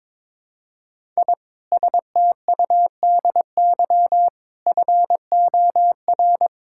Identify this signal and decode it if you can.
Morse code, heard as a keyed tone, about 22 wpm, 700 Hz. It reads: I STUDY FOR
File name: Sentences with 7-Word Phrases - 22wpm